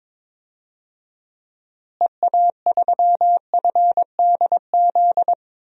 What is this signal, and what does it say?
Morse code, heard as a keyed tone, about 21 wpm, 700 Hz. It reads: EA3FDZ
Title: DXCC-100 Call Signs - 6 Characters - Repeated 3x - 22wpm